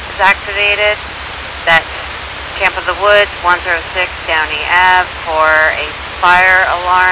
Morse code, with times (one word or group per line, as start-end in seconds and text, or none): none